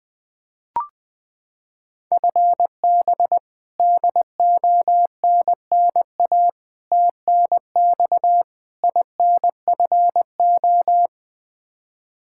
Morse code, heard as a keyed tone, about 20 wpm, 700 Hz. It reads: FB DONNA TNX INFO
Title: Practice QSO - Exchange 3 - More Info - 20wpm